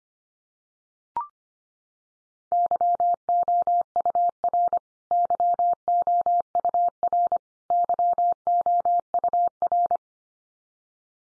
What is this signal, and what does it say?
Morse code, heard as a keyed tone, about 25 wpm, 700 Hz. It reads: YOUR YOUR YOUR
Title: Top 100 Words 25wpm x3